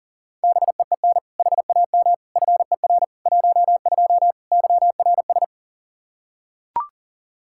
Morse code, heard as a keyed tone, about 40 wpm, 700 Hz. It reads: BEEN HAM FER 12 YRS
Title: Practice QSO - Exchange 3 - More Info - 40wpm